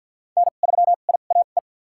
Morse code, heard as a keyed tone, about 49 wpm, spaced as for 35 wpm, 700 Hz. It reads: N3IAE
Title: USA Call Signs - 5 Characters - 2x Character Spacing - 50wpm